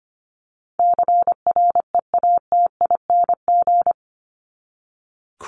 Morse code, heard as a keyed tone, about 24 wpm, 700 Hz. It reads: CREATING
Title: ING Suffix - Top 100 Words with Base Words 25wpm